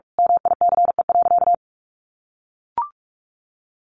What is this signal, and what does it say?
Morse code, heard as a keyed tone, about 44 wpm, 700 Hz. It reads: MICE.
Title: A Sound Of Thunder 45wpm